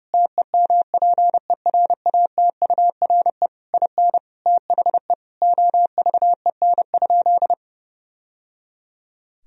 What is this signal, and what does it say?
Morse code, heard as a keyed tone, about 32 wpm, 700 Hz. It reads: TEMPERATURE IN THE OVEN?
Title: Sentences from Top 1000 Words - Review Part 4 - 30wpm